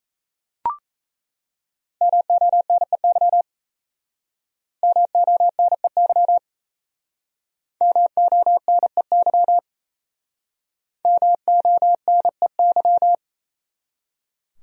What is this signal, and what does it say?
Morse code, heard as a keyed tone, about 37 wpm, 700 Hz. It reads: MONEY MONEY MONEY MONEY